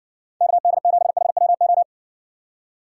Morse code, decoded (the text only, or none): KD7HWO